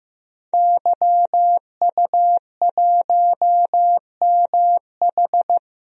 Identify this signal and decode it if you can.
Morse code, heard as a keyed tone, about 16 wpm, 700 Hz. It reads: YU1MH
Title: DXCC-100 Call Signs - 5 Characters - Repeated 3x - 15wpm